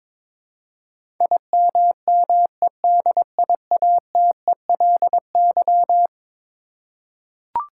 Morse code, heard as a keyed tone, about 21 wpm, 700 Hz. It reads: IMMEDIATELY
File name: practice / 22wpm / Eleven Letter Words - 22wpm